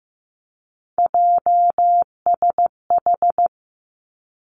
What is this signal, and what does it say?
Morse code, heard as a keyed tone, about 15 wpm, 700 Hz. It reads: JSH
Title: Three Challenging Letter-Number Combinations 15wpm